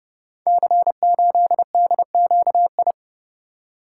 Morse code, heard as a keyed tone, about 30 wpm, 700 Hz. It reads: C8DQI
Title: Code Groups - 5 Letter:Numbers - 30wpm